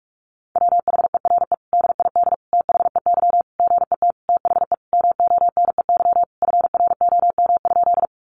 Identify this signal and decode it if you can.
Morse code, heard as a keyed tone, about 46 wpm, 700 Hz. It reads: WHERE DID THEY GET THE MONEY FROM?
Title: Sentences from Top 400 Words 45wpm